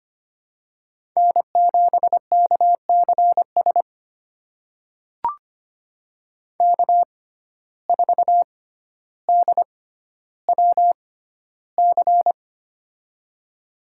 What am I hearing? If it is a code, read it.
Morse code, heard as a keyed tone, about 25 wpm, spaced as for 13 wpm, 700 Hz. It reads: N7KCS K4DWC
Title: USA Call Signs - 5 Characters - 6x Character Spacing - 25wpm